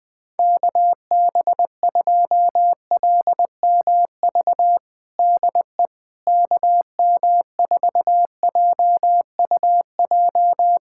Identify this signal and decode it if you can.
Morse code, heard as a keyed tone, about 20 wpm, 700 Hz. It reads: KB2LMV DE KM4JUJ
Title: USA Callsign DE Callsign - 6 Characters - 20wpm